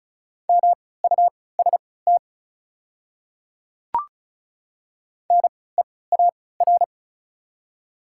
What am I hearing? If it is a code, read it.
Morse code, heard as a keyed tone, about 34 wpm, spaced as for 20 wpm, 690 Hz. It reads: MUST NEAR
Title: 3x Character Spacing - Top 200 Words - 35wpm